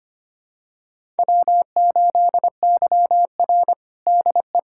Morse code, heard as a keyed tone, about 25 wpm, 700 Hz. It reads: W8YR DE
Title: USA Callsign DE Callsign - 4 Characters - 25wpm